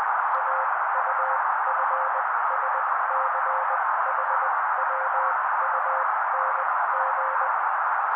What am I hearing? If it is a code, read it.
Morse code, heard as a keyed tone, about 20 wpm, spaced as for 16 wpm, 530 Hz. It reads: AUFSCHWUNG
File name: daily / 16wpm noise